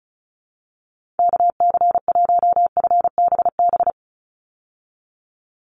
Morse code, heard as a keyed tone, about 34 wpm, 700 Hz. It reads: KC1FBB
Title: USA Call Signs 35wpm